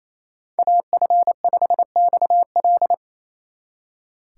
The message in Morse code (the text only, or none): AF5XL